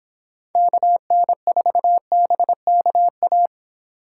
Morse code, heard as a keyed tone, about 26 wpm, 700 Hz. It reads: KN4BKA
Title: Speed Racing - USA 2x3 Call Signs 22wpm